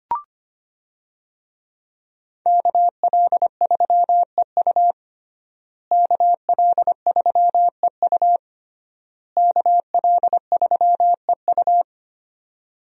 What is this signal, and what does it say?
Morse code, heard as a keyed tone, about 25 wpm, 700 Hz. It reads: KL3EU KL3EU KL3EU
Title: USA Call Signs - 5 Characters - Repeated 3x - 25wpm